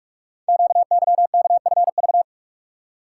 Morse code, heard as a keyed tone, about 44 wpm, 700 Hz. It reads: QYKFV
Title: Code Groups - 5 Letters - 45wpm